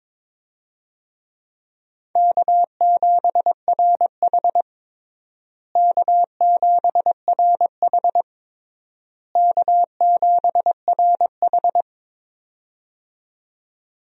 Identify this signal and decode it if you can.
Morse code, heard as a keyed tone, about 22 wpm, 700 Hz. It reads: K7RH K7RH K7RH
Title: USA Call Signs - 4 Characters - Repeated 3x - 22wpm